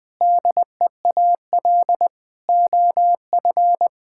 Morse code, heard as a keyed tone, about 20 wpm, 700 Hz. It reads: DEAL OF
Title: Sentences from Top 1000 Words - Review Part 6 - 20wpm